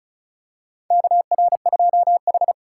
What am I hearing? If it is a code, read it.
Morse code, heard as a keyed tone, about 36 wpm, 700 Hz. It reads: KR2H
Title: USA Call Signs - 4 Characters - 3x Character Spacing - 35wpm